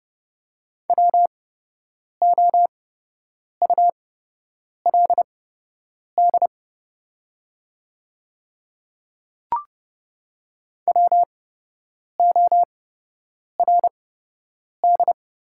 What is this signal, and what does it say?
Morse code, heard as a keyed tone, about 30 wpm, spaced as for 8 wpm, 700 Hz. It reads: WOULD WORD
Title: 8x Character Spacing - Top 300 Words - 30wpm